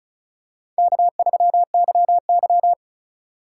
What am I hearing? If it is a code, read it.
Morse code, heard as a keyed tone, about 36 wpm, 700 Hz. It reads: K3YY
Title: USA Call Signs - 4 Characters - 2x Character Spacing - 35wpm